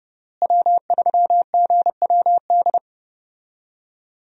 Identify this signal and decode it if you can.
Morse code, heard as a keyed tone, about 29 wpm, 700 Hz. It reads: W3GWD
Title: USA Call Signs - 5 Characters - Repeated 2x - 30wpm